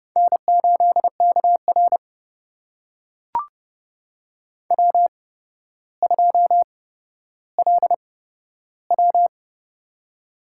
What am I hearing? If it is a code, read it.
Morse code, heard as a keyed tone, about 30 wpm, spaced as for 13 wpm, 700 Hz. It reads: N8KR W2LW
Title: USA Call Signs - 4 Characters - 8x Character Spacing - 30wpm